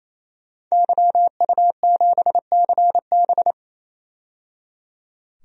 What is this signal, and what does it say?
Morse code, heard as a keyed tone, about 27 wpm, 700 Hz. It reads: YU7CB
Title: DXCC-100 Call Signs - 5 Characters - 28wpm